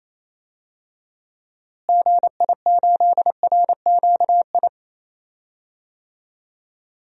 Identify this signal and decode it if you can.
Morse code, heard as a keyed tone, about 28 wpm, 700 Hz. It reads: GI8RQI
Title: DXCC-100 Call Signs - 6 Characters - Repeated 2x - 28wpm